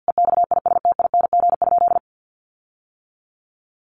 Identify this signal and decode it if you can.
Morse code, heard as a keyed tone, about 50 wpm, 700 Hz. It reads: EXISTING?
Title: Sentences from Top 1500 Words 50wpm